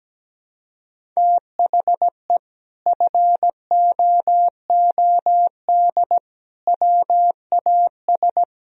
Morse code, heard as a keyed tone, about 17 wpm, 700 Hz. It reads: THE FOOD WAS
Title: Sentences from Top 1000 Words - Review Part 1 - 17wpm